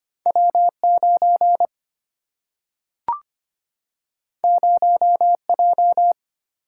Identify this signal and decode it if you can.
Morse code, heard as a keyed tone, about 25 wpm, 700 Hz. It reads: W 9 0 J
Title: Single-Letter Single-Number 25wpm